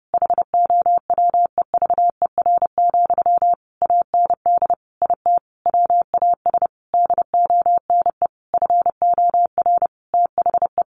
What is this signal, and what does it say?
Morse code, heard as a keyed tone, about 30 wpm, 700 Hz. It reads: HOWEVER, AND IT WAS DONE FOR THE
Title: Sentences from Top 600 Words 30wpm